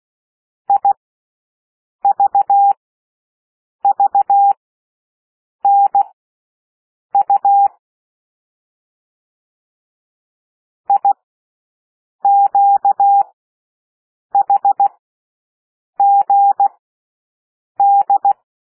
Morse code, heard as a keyed tone, about 16 wpm, spaced as for 6 wpm, 800 Hz. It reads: IVVNU IQHGD
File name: cwmp3 / L1 SOUND LETTERS 16 wpm